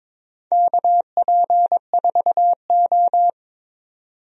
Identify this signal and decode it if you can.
Morse code, heard as a keyed tone, about 22 wpm, 700 Hz. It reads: KP4O
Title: DXCC-100 Call Signs - 4 Characters - 22wpm